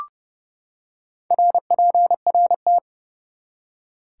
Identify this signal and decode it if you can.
Morse code, heard as a keyed tone, about 30 wpm, 700 Hz. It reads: RPRT
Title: Top 180 QSO Elements 30wpm